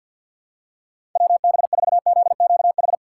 Morse code, heard as a keyed tone, about 47 wpm, 700 Hz. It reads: WB4ZQH
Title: USA Call Signs - 6 Characters - 3x Character Spacing - 50wpm